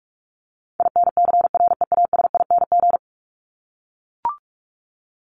Morse code, heard as a keyed tone, about 44 wpm, 700 Hz. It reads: INCREASING
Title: ING Suffix - Top 100 Words with Base Words 45wpm